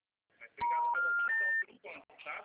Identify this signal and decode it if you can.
Special information tone, with speech.
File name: out- -2011